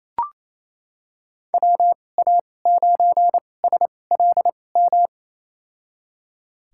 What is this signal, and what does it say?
Morse code, heard as a keyed tone, about 28 wpm, spaced as for 20 wpm, 700 Hz. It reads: WA9SLM
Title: USA Call Signs - 2x Character Spacing - 28wpm